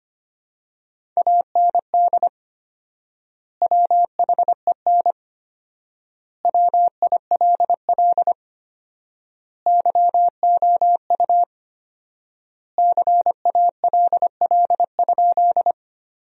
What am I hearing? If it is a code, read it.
Morse code, heard as a keyed tone, about 25 wpm, 700 Hz. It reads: AND WHEN WILL YOU CALL?
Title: 4x Word-Spacing - Sentences from Top 100 Words - 25wpm